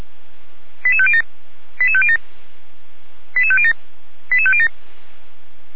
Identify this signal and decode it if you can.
Five-tone selective call (ZVEI), heard as quick runs of five tones, tones of about 70 ms. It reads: 88507 88507 88507 88507
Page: /455/SDR/ZVEI/88505/2021.11.01/